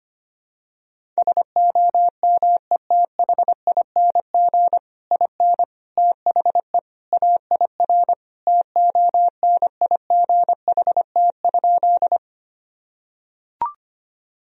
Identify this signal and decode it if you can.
Morse code, heard as a keyed tone, about 25 wpm, 700 Hz. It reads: SOMETHING IN THE AIR TONIGHT?
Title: Sentences from Top 1000 Words - Review Part 1 - 25wpm